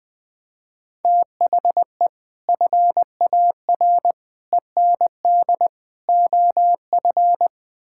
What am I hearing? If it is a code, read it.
Morse code, heard as a keyed tone, about 20 wpm, 700 Hz. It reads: THE FAR END OF
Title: Build A Phrase - 5-Word Phrases - 20wpm